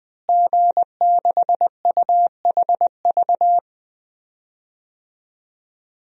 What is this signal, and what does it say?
Morse code, heard as a keyed tone, about 20 wpm, 700 Hz. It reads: G6UHV
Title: DXCC-100 Call Signs - 5 Characters - Repeated 3x - 20wpm